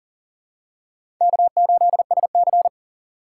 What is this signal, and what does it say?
Morse code, heard as a keyed tone, about 40 wpm, 700 Hz. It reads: K8SC
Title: USA Call Signs - 6x Character Spacing - 40wpm